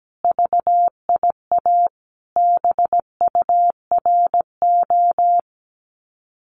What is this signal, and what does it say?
Morse code, heard as a keyed tone, about 17 wpm, 700 Hz. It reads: VIA BURO
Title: Common QSO Phrases 17wpm